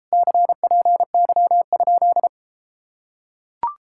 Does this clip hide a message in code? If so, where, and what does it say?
Yes: Morse code from 0.1 to 2.3 s, heard as a keyed tone, about 33 wpm, 700 Hz. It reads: CPY?